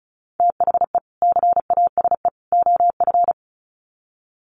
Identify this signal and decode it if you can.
Morse code, heard as a keyed tone, about 34 wpm, 700 Hz. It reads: THE CASE OF
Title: Sets of 3 Words 35wpm